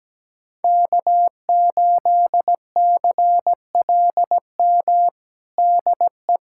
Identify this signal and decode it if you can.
Morse code, heard as a keyed tone, about 17 wpm, 700 Hz. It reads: K8CLM DE
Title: USA Callsign DE Callsign - 5 Characters - 17wpm